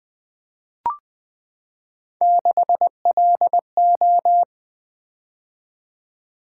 Morse code, heard as a keyed tone, about 20 wpm, 700 Hz. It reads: 6LO
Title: Three Letter-Number Combinations - 20wpm